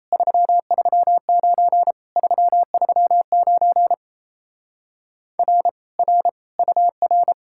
Morse code, heard as a keyed tone, about 37 wpm, 700 Hz. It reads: 339 339 R R UR